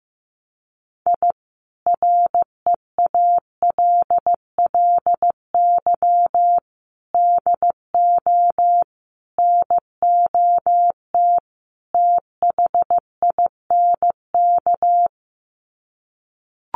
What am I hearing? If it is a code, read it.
Morse code, heard as a keyed tone, about 15 wpm, 700 Hz. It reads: I REALLY DO NOT THINK
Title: Sets of 5 Words Encore2 15wpm